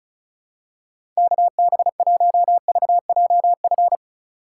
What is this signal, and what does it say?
Morse code, heard as a keyed tone, about 34 wpm, 700 Hz. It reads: KB1VJF
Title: USA Call Signs - 6 Characters - 2x Character Spacing - 35wpm